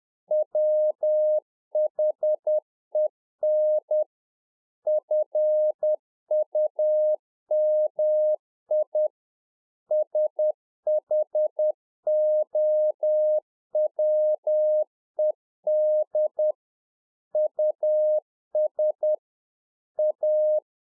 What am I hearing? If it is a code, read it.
Morse code, heard as a keyed tone, about 10 wpm, 610 Hz. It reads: WHEN FUMI SHOWED US A